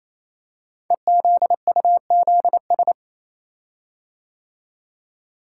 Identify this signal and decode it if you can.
Morse code, heard as a keyed tone, about 27 wpm, 690 Hz. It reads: EZUZS
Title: Code Groups - 5 Letters - 28wpm